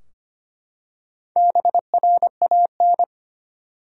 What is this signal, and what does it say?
Morse code, heard as a keyed tone, about 25 wpm, 700 Hz. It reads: BRAN
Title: USA Common Names - Max 4 Letters - 25wpm